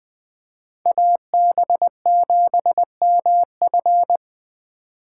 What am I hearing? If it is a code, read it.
Morse code, heard as a keyed tone, about 20 wpm, 700 Hz. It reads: AB7MF
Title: Speed Racing - USA 2x2 Call Signs 17wpm